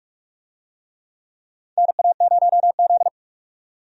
Morse code, heard as a keyed tone, about 43 wpm, 700 Hz. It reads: NA0Z
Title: USA Call Signs - 4 Characters - 45wpm